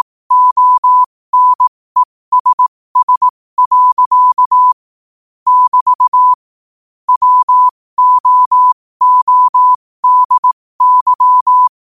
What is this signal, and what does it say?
Morse code, heard as a keyed tone, about 18 wpm, spaced as for 15 wpm, 1000 Hz. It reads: ONESS. <BT> WOODY
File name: qotdfiles / 15wpm